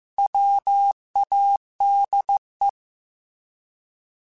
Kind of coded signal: Morse code